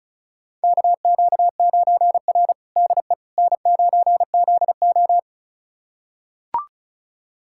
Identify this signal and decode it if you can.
Morse code, heard as a keyed tone, about 36 wpm, 700 Hz. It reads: KQ9R DE N9ZO
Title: USA Callsign DE Callsign - 4 Characters - 35wpm